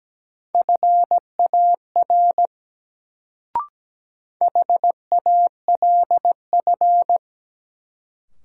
Morse code, heard as a keyed tone, about 17 wpm, 700 Hz. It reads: FAR HALF
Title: Sets of 3 Words - New Word Drill - 17wpm